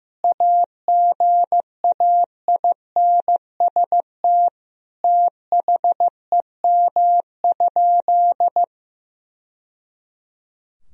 Morse code, heard as a keyed tone, about 15 wpm, 700 Hz. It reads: AGAINST THEM?
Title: Sentences from Top 1000 Words - Review Part 2 - 15wpm